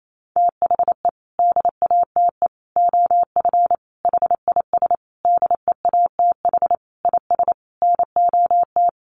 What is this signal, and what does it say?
Morse code, heard as a keyed tone, about 28 wpm, 700 Hz. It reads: THE DATE OF HIS DEATH IS NOT
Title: Sentences from Top 1100 Words 28wpm